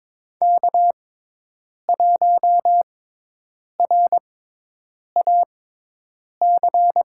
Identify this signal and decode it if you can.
Morse code, heard as a keyed tone, about 23 wpm, spaced as for 8 wpm, 700 Hz. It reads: K1RAC
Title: USA Call Signs - 5 Characters - 6x Character Spacing - 22wpm